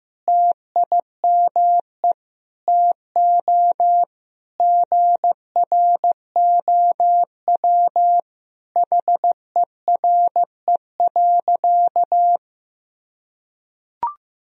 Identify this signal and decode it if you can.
Morse code, heard as a keyed tone, about 16 wpm, 700 Hz. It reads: TIME TO GROW HERE.
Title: Sentences from Top 1000 Words - Review Part 1 - 15wpm